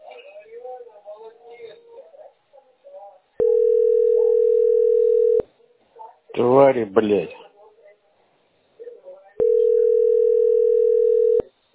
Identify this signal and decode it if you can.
Ringback tone (North America), 440 + 480 Hz, with speech.
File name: b88e0c8a-0cff-4aae-b069